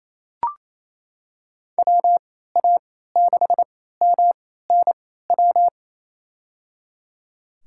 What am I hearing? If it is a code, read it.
Morse code, heard as a keyed tone, about 28 wpm, spaced as for 16 wpm, 700 Hz. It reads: WA6MNW
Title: USA Call Signs - 6 Characters - 3x Character Spacing - 28wpm